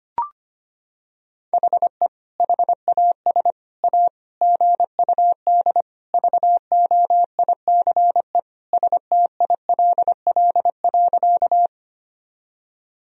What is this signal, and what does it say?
Morse code, heard as a keyed tone, about 25 wpm, 700 Hz. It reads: HE HAS A GUD VOICE STILL.
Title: Sentences from Top 500 Words and Abbreviations 25wpm